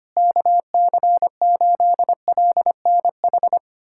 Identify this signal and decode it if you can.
Morse code, heard as a keyed tone, about 25 wpm, 700 Hz. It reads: KC8LNH
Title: USA Callsign DE Callsign 25wpm